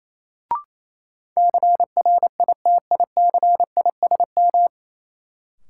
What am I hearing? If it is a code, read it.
Morse code, heard as a keyed tone, about 28 wpm, 700 Hz. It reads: CRITICISM
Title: Sentences from Top 1300 Words - New Word Drill - 28wpm